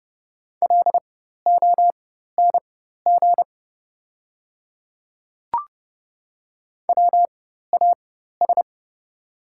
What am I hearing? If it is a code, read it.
Morse code, heard as a keyed tone, about 30 wpm, spaced as for 14 wpm, 700 Hz. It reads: LONG WAS
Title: 4x Character Spacing - Top 100 Words - 30wpm